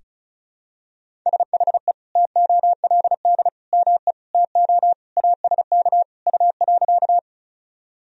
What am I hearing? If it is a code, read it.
Morse code, heard as a keyed tone, about 34 wpm, 700 Hz. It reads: SHE TOLD ME TO ASK U.
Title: Sentences from Top 400 Words and Abbreviations 35wpm